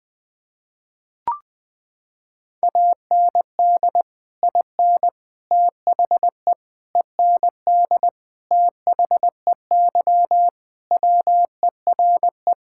Morse code, heard as a keyed tone, about 20 wpm, 700 Hz. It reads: AND IN THE END THEY WERE
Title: Sentences with 2-Word Phrases - 20wpm